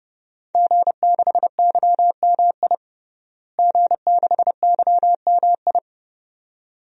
Morse code, heard as a keyed tone, about 31 wpm, 700 Hz. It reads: G6YMI G6YMI
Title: DXCC-100 Call Signs - 5 Characters - Repeated 2x - 30wpm